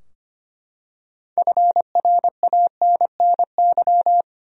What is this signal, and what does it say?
Morse code, heard as a keyed tone, about 25 wpm, 700 Hz. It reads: FRANNY
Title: USA Common Names - 25wpm